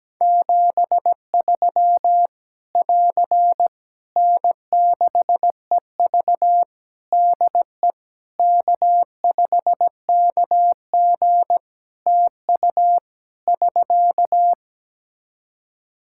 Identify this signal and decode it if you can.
Morse code, heard as a keyed tone, about 17 wpm, 700 Hz. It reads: 73 <AR> N6EV DE K5KG TU <SK>
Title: K1USN SST QSO Practice 17wpm